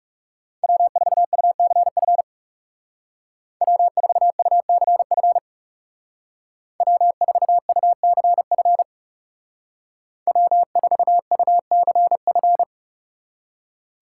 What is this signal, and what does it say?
Morse code, heard as a keyed tone, about 36 wpm, 700 Hz. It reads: W4UCF W4UCF W4UCF W4UCF
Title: Speed Racing - USA 1x3 Call Signs 30wpm